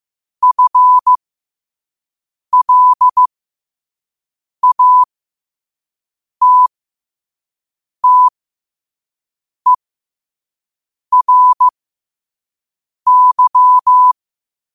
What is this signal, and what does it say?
Morse code, heard as a keyed tone, about 15 wpm, spaced as for 5 wpm, 1000 Hz. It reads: FLATTERY
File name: qotdfiles / 5wpm